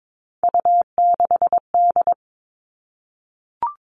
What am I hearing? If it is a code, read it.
Morse code, heard as a keyed tone, about 22 wpm, 700 Hz. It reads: U6D
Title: All Cut Numbers - 3-Digit 22wpm